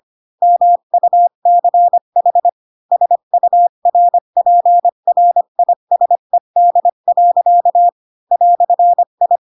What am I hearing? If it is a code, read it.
Morse code, heard as a keyed tone, about 25 wpm, 700 Hz. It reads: MUCH SURPRISED. "I
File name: Pinocchio / Pinocchio ch7 20 25 ik1hge